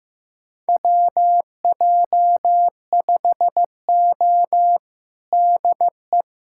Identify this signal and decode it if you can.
Morse code, heard as a keyed tone, about 15 wpm, 700 Hz. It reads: WJ5O DE
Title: USA Callsign DE Callsign - 4 Characters - 15wpm